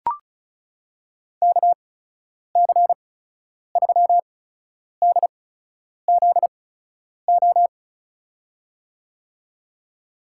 Morse code, heard as a keyed tone, about 36 wpm, spaced as for 10 wpm, 700 Hz. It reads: KC3DZO
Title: USA Call Signs - 6 Characters - 8x Character Spacing - 35wpm